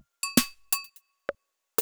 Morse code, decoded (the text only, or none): none